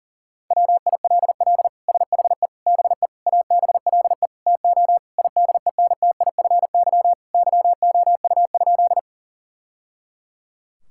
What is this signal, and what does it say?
Morse code, heard as a keyed tone, about 37 wpm, 700 Hz. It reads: WILL SHE BE ABLE TO IDENTIFY YOU?
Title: Sentences from Top 1000 Words - Review Part 6 - 40wpm